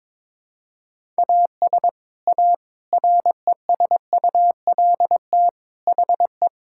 Morse code, heard as a keyed tone, about 22 wpm, 700 Hz. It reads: AS A RESULT HE
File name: practice / 22wpm / Sentences with 3-Word Phrases - 22wpm